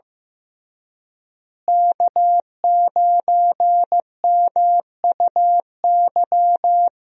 Morse code, heard as a keyed tone, about 16 wpm, 700 Hz. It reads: K9MUY
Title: USA Call Signs - 5 Characters - Repeated 3x - 15wpm